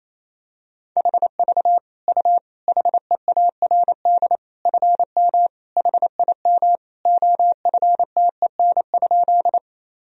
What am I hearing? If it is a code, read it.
Morse code, heard as a keyed tone, about 28 wpm, 700 Hz. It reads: HV U HEARD FM HIM OFTEN?
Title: Sentences from Top 400 Words and Abbreviations 28wpm